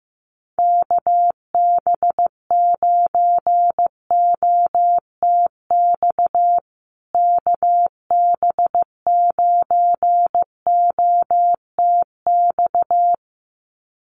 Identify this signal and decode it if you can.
Morse code, heard as a keyed tone, about 15 wpm, 700 Hz. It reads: KB9OTX KB9OTX
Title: USA Call Signs Repeated 2x 15wpm